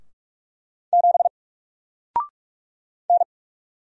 Morse code, heard as a keyed tone, about 44 wpm, 700 Hz. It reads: 7 N
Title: Single Character (Letter:Number:Punctuation:Prosign) Alt 45wpm